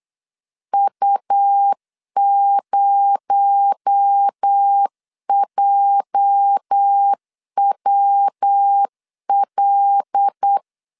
Morse code, heard as a keyed tone, about 8 wpm, 800 Hz. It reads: U0JWL
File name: LEZIONI / 8 YJ45 / 8 YJ 8wpm 3313